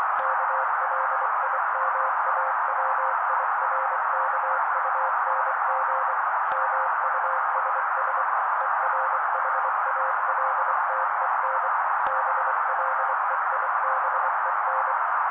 Morse code, heard as a keyed tone, about 24 wpm, 530 Hz. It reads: KLIMAWIRKUNG MUSS ERHALTEN BLEIBEN
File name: daily / 23wpm noise